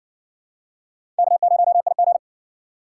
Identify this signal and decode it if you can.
Morse code, heard as a keyed tone, about 60 wpm, 700 Hz. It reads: K0IG